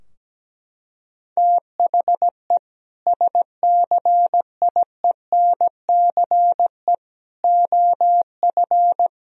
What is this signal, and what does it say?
Morse code, heard as a keyed tone, about 17 wpm, 700 Hz. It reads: THE SCIENCE OF
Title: Sentences from Top 1000 Words - Review Part 2 - 17wpm